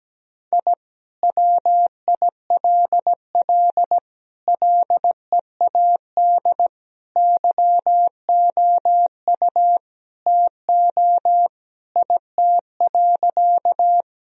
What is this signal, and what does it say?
Morse code, heard as a keyed tone, about 17 wpm, 700 Hz. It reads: I WILL LEAD YOU TO IT.
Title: Sentences from Top 1000 Words - Review Part 2 - 17wpm